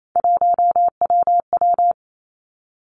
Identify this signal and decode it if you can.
Morse code, heard as a keyed tone, about 28 wpm, 700 Hz. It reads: 1WW
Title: Three Challenging Letter-Number Combinations 28wpm